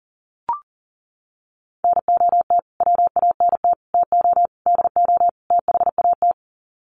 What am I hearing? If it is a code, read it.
Morse code, heard as a keyed tone, about 40 wpm, 700 Hz. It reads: NOT WANT TO DO THAT
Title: Sets of 5 Words Encore 40wpm